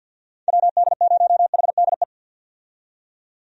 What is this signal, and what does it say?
Morse code, heard as a keyed tone, about 51 wpm, 700 Hz. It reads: WD0HDE
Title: USA Call Signs - 6 Characters - 50wpm